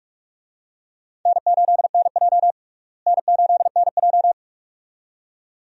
Morse code, heard as a keyed tone, about 46 wpm, 700 Hz. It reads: N8NJ N8NJ
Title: USA Call Signs - 4 Characters - Repeated 2x - 45wpm